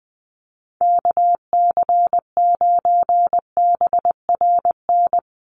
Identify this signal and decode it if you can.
Morse code, heard as a keyed tone, about 20 wpm, 700 Hz. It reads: KC9BRN
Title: USA Call Signs 20wpm